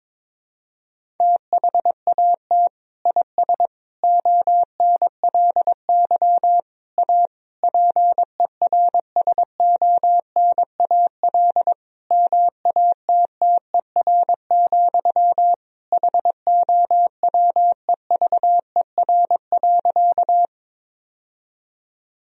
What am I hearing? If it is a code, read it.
Morse code, heard as a keyed tone, about 22 wpm, 700 Hz. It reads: THAT IS ONLY A PERSONAL MATTER, HOWEVER.
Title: Sentences from Top 1000 Words - Review Part 4 - 22wpm